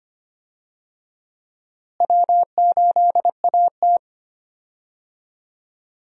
Morse code, heard as a keyed tone, about 25 wpm, 700 Hz. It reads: W8AT
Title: USA Call Signs - 4 Characters - Repeated 2x - 25wpm